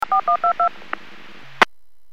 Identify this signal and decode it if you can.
DTMF, heard as telephone touch tones, tones of about 100 ms. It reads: 4122
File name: WPS 451.25